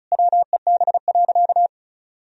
Morse code, heard as a keyed tone, about 36 wpm, 700 Hz. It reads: WEB.
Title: Sentences from Top 1000 Words - Review Part 4 - 35wpm